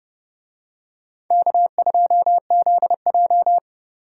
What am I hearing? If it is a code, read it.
Morse code, heard as a keyed tone, about 29 wpm, 700 Hz. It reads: K2ZJ